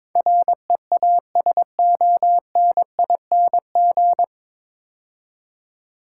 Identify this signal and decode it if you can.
Morse code, heard as a keyed tone, about 22 wpm, 700 Hz. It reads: REASONING
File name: 9 Letter Words - First 3 Letters Given - 22wpm